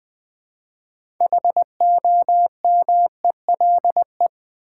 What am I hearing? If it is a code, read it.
Morse code, heard as a keyed tone, about 20 wpm, 700 Hz. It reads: HOMELE
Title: Build A Word - 2 Letters At A Time - 8-Letter Words - 20wpm